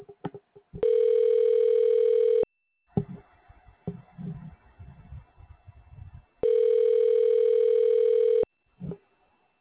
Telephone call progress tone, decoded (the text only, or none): ringback tone